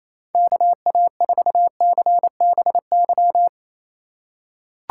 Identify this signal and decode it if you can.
Morse code, heard as a keyed tone, about 28 wpm, 700 Hz. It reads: KA4CBY
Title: USA Call Signs - 6 Characters - 4x Character Spacing - 28wpm